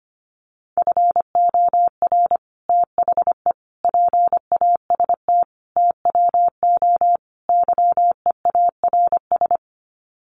Morse code, heard as a keyed tone, about 25 wpm, 700 Hz. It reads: FOR THE PAST TWO YEARS
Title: Sets of 5 Words Encore2 25wpm